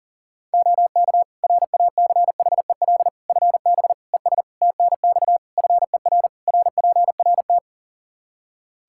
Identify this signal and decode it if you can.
Morse code, heard as a keyed tone, about 40 wpm, 700 Hz. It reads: OK RACHEL FB ES TNX FER RPRT
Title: K1USN SST QSO Practice 40wpm